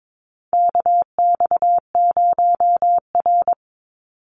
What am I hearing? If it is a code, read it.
Morse code, heard as a keyed tone, about 22 wpm, 700 Hz. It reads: KX0R
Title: NA SOTA Activator Callsigns 22wpm